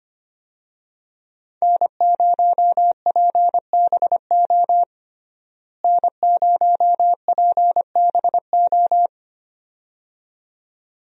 Morse code, heard as a keyed tone, about 25 wpm, 700 Hz. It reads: N0PBO N0PBO
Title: USA Call Signs - 5 Characters - Repeated 2x - 25wpm